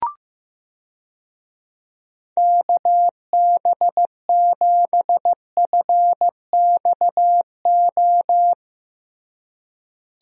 Morse code, heard as a keyed tone, about 16 wpm, 700 Hz. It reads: KB7FXO
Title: USA Call Signs - 6 Characters - Repeated 2x - 15wpm